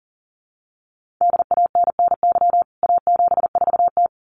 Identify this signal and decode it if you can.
Morse code, heard as a keyed tone, about 40 wpm, 700 Hz. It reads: DANNY A74T
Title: CWT Exchange 40wpm